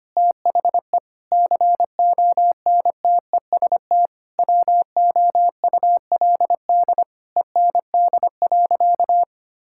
Morse code, heard as a keyed tone, about 25 wpm, 700 Hz. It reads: THE CONTEST WOULD END.
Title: Sentences from Top 1600 Words 25wpm